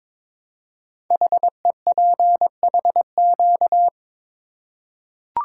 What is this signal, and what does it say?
Morse code, heard as a keyed tone, about 21 wpm, 700 Hz. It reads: HEPHQ
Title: Code Groups - 5 Letters - 22wpm